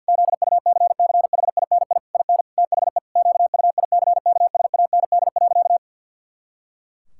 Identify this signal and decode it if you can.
Morse code, heard as a keyed tone, about 51 wpm, 700 Hz. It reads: ZUCCHINI IN THE QUICKSAND.
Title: Pangram Challenge v2 - 50wpm